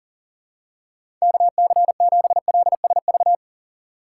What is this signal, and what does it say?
Morse code, heard as a keyed tone, about 40 wpm, 700 Hz. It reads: KC7LSV
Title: USA Call Signs - 6 Characters - 2x Character Spacing - 40wpm